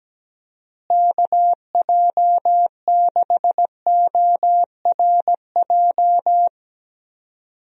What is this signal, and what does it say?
Morse code, heard as a keyed tone, about 17 wpm, 700 Hz. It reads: KJ6ORJ
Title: USA Call Signs - 6 Characters - Repeated 2x - 17wpm